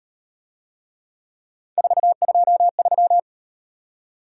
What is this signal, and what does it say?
Morse code, heard as a keyed tone, about 38 wpm, 700 Hz. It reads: 423